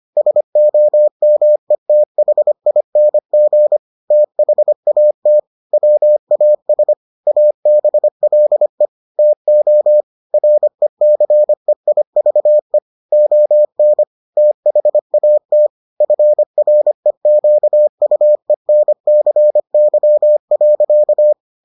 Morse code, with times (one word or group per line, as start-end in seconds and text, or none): 0.2-3.8 SOMETHING
4.1-5.4 THAT
5.7-6.9 WAS
7.3-8.9 ABLE
9.2-10.0 TO
10.3-12.8 RECEIVE
13.1-14.0 ON
14.4-15.7 THAT
16.0-21.3 FREQUENCY.